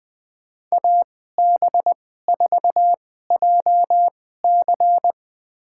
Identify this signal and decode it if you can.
Morse code, heard as a keyed tone, about 20 wpm, spaced as for 14 wpm, 700 Hz. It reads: AB4JC
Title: USA Call Signs - 2x Character Spacing - 20wpm